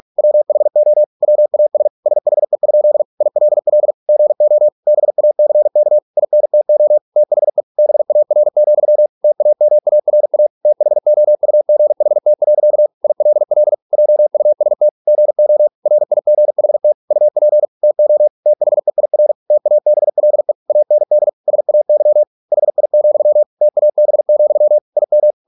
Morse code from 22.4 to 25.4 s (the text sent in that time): HI, TAB, IM